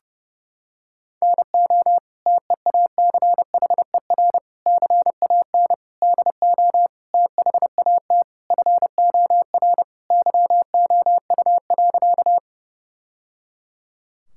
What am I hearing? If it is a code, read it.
Morse code, heard as a keyed tone, about 30 wpm, 700 Hz. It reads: NO TEACHER CAN DO THAT FOR YOU.
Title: Sentences from Top 1000 Words - Review Part 6 - 30wpm